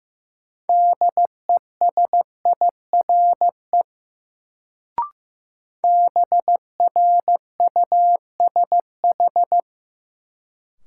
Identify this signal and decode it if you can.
Morse code, heard as a keyed tone, about 15 wpm, 700 Hz. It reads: DESIRE BRUSH
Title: Sentences from Top 1100 Words - New Word Drill - 15wpm